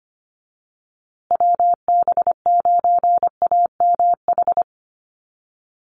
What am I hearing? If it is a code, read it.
Morse code, heard as a keyed tone, about 25 wpm, 700 Hz. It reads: WB9AMH
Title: USA Call Signs Repeated 3x 25wpm